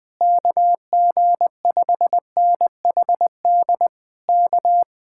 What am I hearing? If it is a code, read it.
Morse code, heard as a keyed tone, about 20 wpm, 700 Hz. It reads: KG5NHD K
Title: Speed-Racing Common QSO Phrases Encore - 20wpm